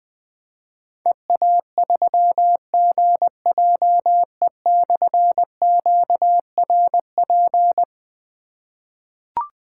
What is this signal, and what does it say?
Morse code, heard as a keyed tone, about 20 wpm, 700 Hz. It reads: EA3GJE/QRP
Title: Difficult Call Signs 20wpm